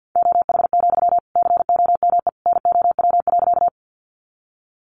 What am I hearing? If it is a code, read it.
Morse code, heard as a keyed tone, about 50 wpm, 700 Hz. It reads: OH, COME NOW.
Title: Sentences from Top 500 Words 50wpm